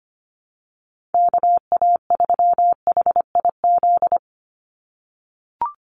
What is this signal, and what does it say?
Morse code, heard as a keyed tone, about 25 wpm, 700 Hz. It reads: KA3HIZ
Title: USA Call Signs 25wpm